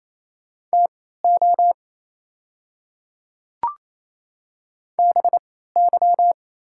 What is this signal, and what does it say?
Morse code, heard as a keyed tone, about 28 wpm, spaced as for 16 wpm, 690 Hz. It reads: TO BY
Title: 3x Character Spacing - Top 100 Words - 28wpm